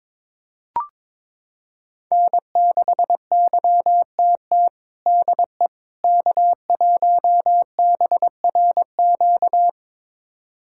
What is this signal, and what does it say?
Morse code, heard as a keyed tone, about 22 wpm, 700 Hz. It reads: N6YTT DE K1BRQ
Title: USA Callsign DE Callsign 22wpm